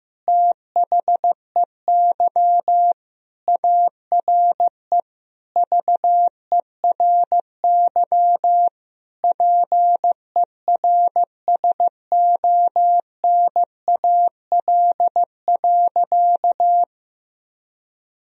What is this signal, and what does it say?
Morse code, heard as a keyed tone, about 16 wpm, 700 Hz. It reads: THEY ARE VERY PERSONAL.
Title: Sentences from Top 1000 Words - Review Part 4 - 15wpm